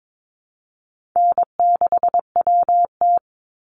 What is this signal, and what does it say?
Morse code, heard as a keyed tone, about 22 wpm, 700 Hz. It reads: N6WT
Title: NA SOTA Chaser Callsigns 22wpm